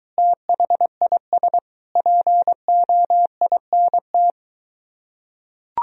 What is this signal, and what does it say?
Morse code, heard as a keyed tone, about 21 wpm, 700 Hz. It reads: THIS POINT